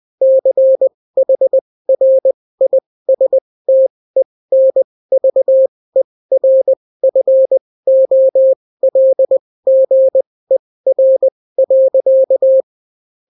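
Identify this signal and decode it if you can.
Morse code, heard as a keyed tone, about 20 wpm, spaced as for 16 wpm, 530 Hz. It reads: CHRISTENVERFOLGER.
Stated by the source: Daily Morse weather and top stories at 16 wpm, clean without QRN.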